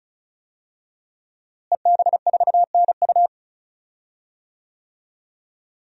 Morse code, heard as a keyed tone, about 34 wpm, 700 Hz. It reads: EB4NU
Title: DXCC-100 Call Signs - 5 Characters - Repeated 2x - 35wpm